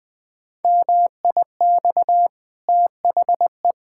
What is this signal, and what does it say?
Morse code, heard as a keyed tone, about 20 wpm, 700 Hz. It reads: MIX THE
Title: Easy Sentences with Hints - 20wpm